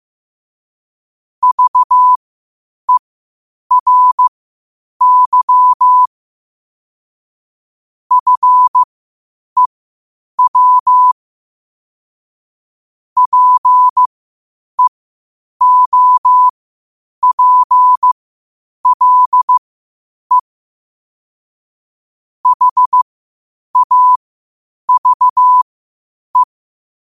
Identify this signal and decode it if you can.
Morse code, heard as a keyed tone, about 15 wpm, spaced as for 8 wpm, 1000 Hz. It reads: VERY FEW PEOPLE HAVE
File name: qotdfiles / 8wpm